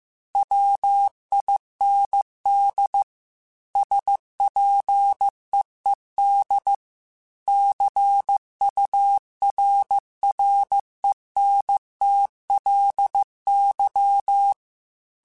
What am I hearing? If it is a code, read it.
Morse code, heard as a keyed tone, about 15 wpm, 800 Hz. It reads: WIND SPEED CURRENTLY
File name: Weather Sets / Weather Sets-mp3s / Weather 5 12 @13